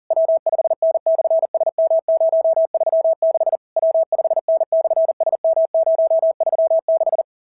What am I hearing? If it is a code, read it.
Morse code, heard as a keyed tone, about 40 wpm, 650 Hz. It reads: W5N/SM036 W5N/SM036
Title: SOTA QSO Practice 40wpm